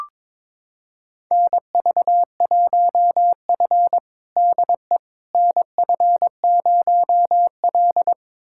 Morse code, heard as a keyed tone, about 21 wpm, 700 Hz. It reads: NV1F DE NF0L
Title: USA Callsign DE Callsign - 4 Characters - 22wpm